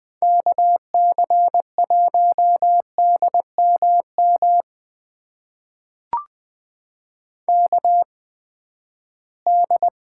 Morse code, heard as a keyed tone, about 19 wpm, spaced as for 14 wpm, 700 Hz. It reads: KC1DMM KD
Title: USA Call Signs - 6 Characters - 8x Character Spacing - 20wpm